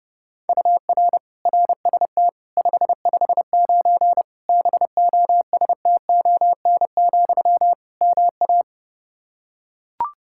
Morse code, heard as a keyed tone, about 30 wpm, 700 Hz. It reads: UR RST 559 BOSTON, MA
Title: K1USN SST QSO Practice 30wpm